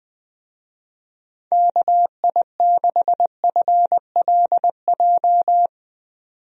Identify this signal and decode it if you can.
Morse code, heard as a keyed tone, about 21 wpm, 700 Hz. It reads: KI6FLJ
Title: USA Call Signs - 6 Characters - 20wpm